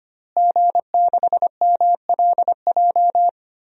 Morse code, heard as a keyed tone, about 25 wpm, 700 Hz. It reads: G6MLJ
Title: DXCC-100 Call Signs - 5 Characters - Repeated 2x - 25wpm